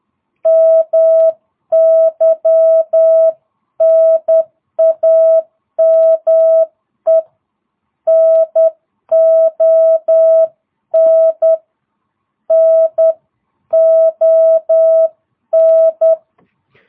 Morse code, heard as a keyed tone, about 10 wpm, 660 Hz. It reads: MY NAME NON NON